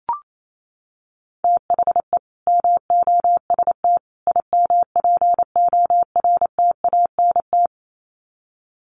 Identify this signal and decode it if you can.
Morse code, heard as a keyed tone, about 28 wpm, 700 Hz. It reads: THE MOST IMPORTANT
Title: Sets of 3 Words 28wpm